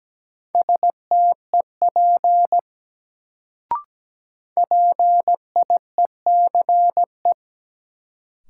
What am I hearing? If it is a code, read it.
Morse code, heard as a keyed tone, about 17 wpm, 700 Hz. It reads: STEP PIECE
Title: Sentences from Top 400 Words - New Word Drill - 17wpm